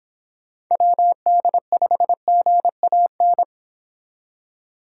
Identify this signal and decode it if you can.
Morse code, heard as a keyed tone, about 26 wpm, 700 Hz. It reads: WD5GAN